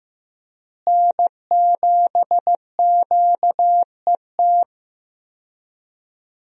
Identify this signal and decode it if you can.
Morse code, heard as a keyed tone, about 15 wpm, 700 Hz. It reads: N7QET
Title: USA Call Signs - 5 Characters - Repeated 2x - 15wpm